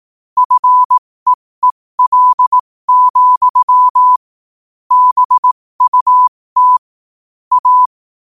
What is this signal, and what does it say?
Morse code, heard as a keyed tone, about 18 wpm, spaced as for 15 wpm, 1000 Hz. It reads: FEEL, BUT A
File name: qotdfiles / 15wpm